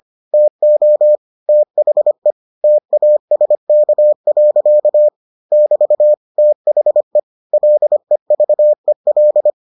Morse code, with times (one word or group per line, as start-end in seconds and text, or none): 0.3-1.2 TO
1.5-2.3 THE
2.6-5.1 TASK.
5.5-6.1 <BT>
6.4-7.2 THE
7.5-9.5 LEVEL